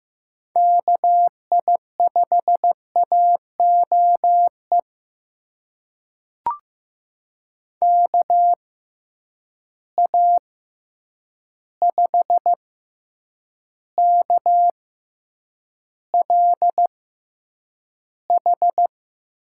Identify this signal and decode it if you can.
Morse code, heard as a keyed tone, about 15 wpm, spaced as for 8 wpm, 700 Hz. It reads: KI5AOE KA5KLH
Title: USA Call Signs - 6 Characters - 6x Character Spacing - 15wpm